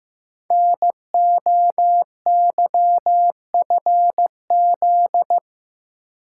Morse code, heard as a keyed tone, about 15 wpm, 700 Hz. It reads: NOYFZ
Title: Code Groups - 5 Letters - 15wpm